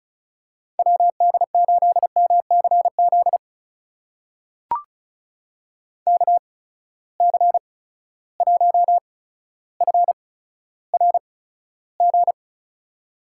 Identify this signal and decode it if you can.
Morse code, heard as a keyed tone, about 34 wpm, spaced as for 15 wpm, 700 Hz. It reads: WD8MCZ KC1FRG
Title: USA Call Signs - 6 Characters - 8x Character Spacing - 35wpm